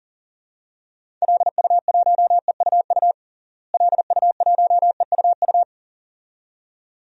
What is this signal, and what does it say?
Morse code, heard as a keyed tone, about 40 wpm, 700 Hz. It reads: LU1EUU LU1EUU
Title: DXCC-100 Call Signs - 6 Characters - Repeated 2x - 40wpm